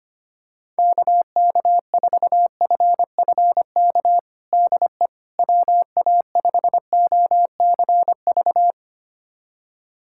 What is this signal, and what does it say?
Morse code, heard as a keyed tone, about 25 wpm, 700 Hz. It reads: KK4FFK DE WA5OCV
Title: USA Callsign DE Callsign - 6 Characters - 25wpm